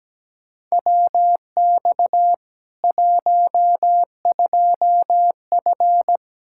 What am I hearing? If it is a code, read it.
Morse code, heard as a keyed tone, about 17 wpm, 700 Hz. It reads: WX 12F
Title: K1USN SST QSO Practice 17wpm